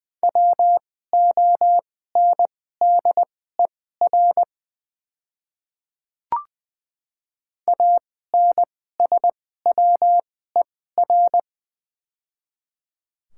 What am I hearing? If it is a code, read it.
Morse code, heard as a keyed tone, about 20 wpm, spaced as for 14 wpm, 700 Hz. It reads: WONDER ANSWER
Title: 2x Character Spacing - Top 500 Words - 20wpm